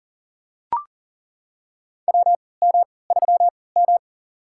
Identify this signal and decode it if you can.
Morse code, heard as a keyed tone, about 39 wpm, spaced as for 22 wpm, 700 Hz. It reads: WM3M
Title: USA Call Signs - 4 Characters - 3x Character Spacing - 40wpm